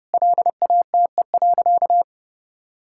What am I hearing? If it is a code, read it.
Morse code, heard as a keyed tone, about 30 wpm, 700 Hz. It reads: LATE.
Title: Sentences from Top 1000 Words - Review Part 6 - 30wpm